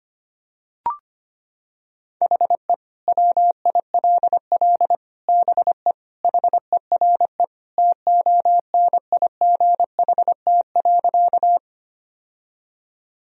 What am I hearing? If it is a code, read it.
Morse code, heard as a keyed tone, about 25 wpm, 700 Hz. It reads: HE WILL BE HERE TONIGHT.
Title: Sentences from Top 1000 Words - Review Part 3 - 25wpm